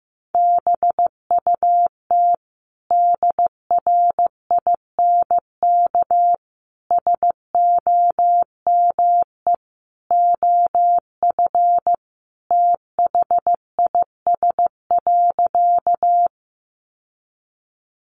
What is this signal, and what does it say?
Morse code, heard as a keyed tone, about 15 wpm, 700 Hz. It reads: BUT DRINK SOME OF THIS.
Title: Sentences from Top 1500 Words 15wpm